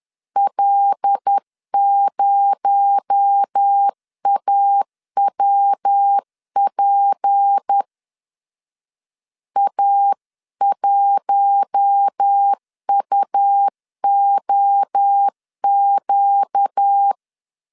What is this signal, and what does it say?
Morse code, heard as a keyed tone, about 10 wpm, 800 Hz. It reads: L0AWP A1UOQ